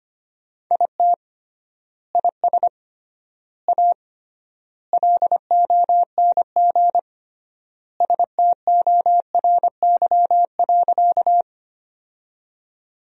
Morse code, heard as a keyed tone, about 25 wpm, 700 Hz. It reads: IT IS A LONG STORY.
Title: 3x Word-Spacing - Sentences from Top 300 Words - 25wpm